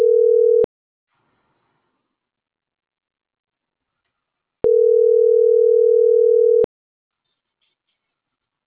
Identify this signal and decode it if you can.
Ringback tone (North America), 440 + 480 Hz.